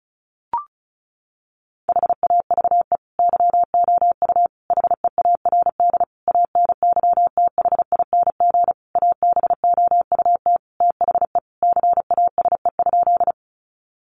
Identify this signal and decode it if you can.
Morse code, heard as a keyed tone, about 34 wpm, 700 Hz. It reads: HAVE YOU HEARD ANYTHING ABOUT THE CASE?
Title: Sentences from Top 1000 Words 35wpm